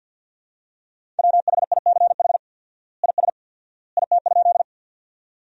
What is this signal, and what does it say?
Morse code, heard as a keyed tone, about 50 wpm, 700 Hz. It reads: WHICH IS IT?
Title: 4x Word-Spacing - Sentences from Top 100 Words - 50wpm